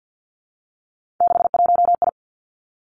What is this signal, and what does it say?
Morse code, heard as a keyed tone, about 50 wpm, 700 Hz. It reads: 61I
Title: Three Challenging Letter-Number Combinations 50wpm